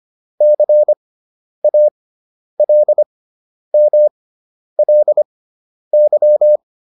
Morse code, heard as a keyed tone, about 25 wpm, spaced as for 10 wpm, 600 Hz. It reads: CALMLY